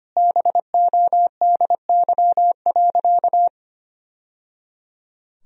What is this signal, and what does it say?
Morse code, heard as a keyed tone, about 25 wpm, 700 Hz. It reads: BODY.
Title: Sentences from Top 1000 Words - Review Part 2 - 25wpm